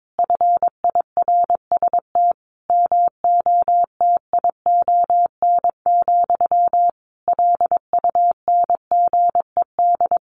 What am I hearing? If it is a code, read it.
Morse code, heard as a keyed tone, about 22 wpm, 700 Hz. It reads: FIRST MOTION, LUNGED
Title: A Sound Of Thunder 22wpm